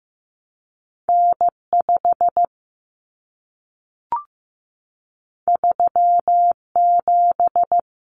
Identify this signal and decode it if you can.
Morse code, heard as a keyed tone, about 15 wpm, 700 Hz. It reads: N5 37
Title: Common Cut Numbers - 2-Digit 15wpm